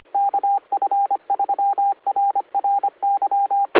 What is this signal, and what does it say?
Morse code, heard as a keyed tone, about 25 wpm, 800 Hz, with radio noise. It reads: KF3RRY